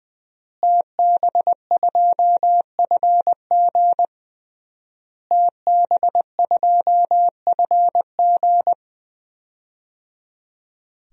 Morse code, heard as a keyed tone, about 20 wpm, 700 Hz. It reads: TB2FG TB2FG
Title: DXCC-100 Call Signs - 5 Characters - Repeated 3x - 20wpm